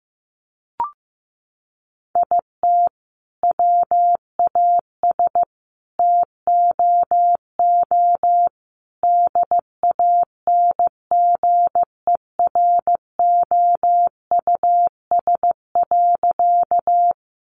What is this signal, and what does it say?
Morse code, heard as a keyed tone, about 15 wpm, 700 Hz. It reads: IT WAS TOO DANGEROUS.
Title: Sentences from Top 1200 Words 15wpm